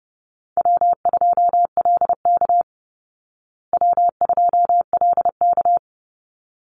Morse code, heard as a keyed tone, about 30 wpm, 700 Hz. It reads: W2LK W2LK
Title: Call Signs Repeated 3x - 30wpm